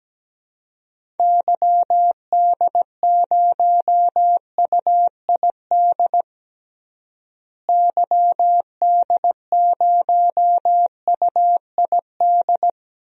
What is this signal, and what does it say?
Morse code, heard as a keyed tone, about 17 wpm, 700 Hz. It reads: YD0UID YD0UID
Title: DXCC-100 Call Signs - 6 Characters - Repeated 3x - 17wpm